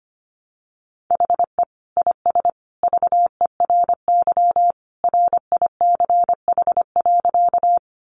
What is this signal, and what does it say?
Morse code, heard as a keyed tone, about 25 wpm, 700 Hz. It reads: HE IS VERY RICH.
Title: Sentences from Top 1100 Words 25wpm